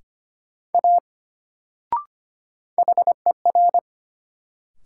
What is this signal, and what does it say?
Morse code, heard as a keyed tone, about 25 wpm, 700 Hz. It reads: A HER
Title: Sentences from Top 100 Words - New Word Drill - 25wpm